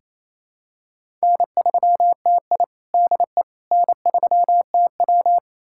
Morse code, heard as a keyed tone, about 28 wpm, 700 Hz. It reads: N3TI DE N3TW
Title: USA Callsign DE Callsign - 4 Characters - 28wpm